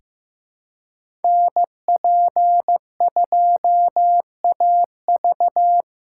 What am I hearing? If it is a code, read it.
Morse code, heard as a keyed tone, about 15 wpm, 700 Hz. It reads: NP2AV
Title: USA Callsign DE Callsign 15wpm